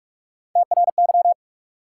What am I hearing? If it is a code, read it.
Morse code, heard as a keyed tone, about 46 wpm, 700 Hz. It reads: TRY
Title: Sets of 2 Words - New Word Drill - 45wpm